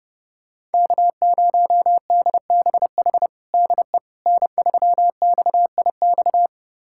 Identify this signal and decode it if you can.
Morse code, heard as a keyed tone, about 30 wpm, 700 Hz. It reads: K0DBH DE N3XIX
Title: USA Callsign DE Callsign - 5 Characters - 30wpm